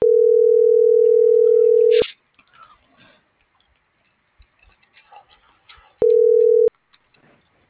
Ringback tone (North America), 440 + 480 Hz.